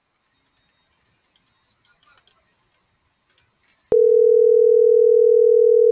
Ringback tone (North America), 440 + 480 Hz.